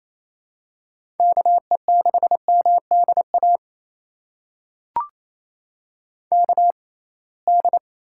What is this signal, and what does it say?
Morse code, heard as a keyed tone, about 28 wpm, spaced as for 21 wpm, 700 Hz. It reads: KE6MDA KD
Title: USA Call Signs - 6 Characters - 6x Character Spacing - 28wpm